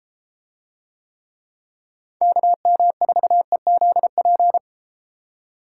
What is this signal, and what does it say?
Morse code, heard as a keyed tone, about 33 wpm, 700 Hz. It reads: KM4EZP